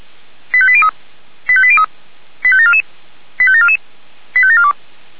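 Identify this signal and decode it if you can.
Five-tone selective call (ZVEI), heard as quick runs of five tones, tones of about 70 ms. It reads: 85803 85803 85744 85744 85742
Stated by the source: Five-tone ZVEI selective call received by SDR.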